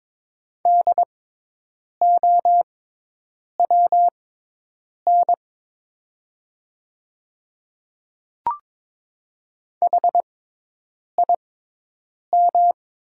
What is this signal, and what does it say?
Morse code, heard as a keyed tone, about 22 wpm, spaced as for 8 wpm, 700 Hz. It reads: DOWN HIM
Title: 6x Character Spacing - Top 500 Words - 22wpm